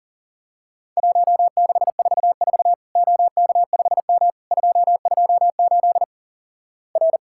Morse code, heard as a keyed tone, about 38 wpm, 700 Hz. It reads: 1644 OK5M 228 R
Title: Worked All Europe DX Contest - QTC Exchange - 40wpm